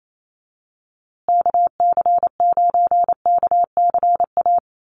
Morse code, heard as a keyed tone, about 28 wpm, 700 Hz. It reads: KC9KCA
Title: USA Call Signs Repeated 3x 28wpm